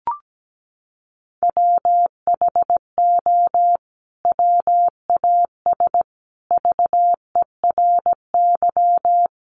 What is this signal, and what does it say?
Morse code, heard as a keyed tone, about 17 wpm, 700 Hz. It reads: WHO WAS VERY
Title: Sentences from Top 1400 Words 17wpm